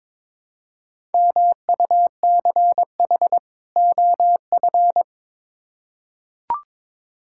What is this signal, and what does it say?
Morse code, heard as a keyed tone, about 21 wpm, 700 Hz. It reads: MUCH OF
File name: Build A Phrase - 2-Word Phrases - 22wpm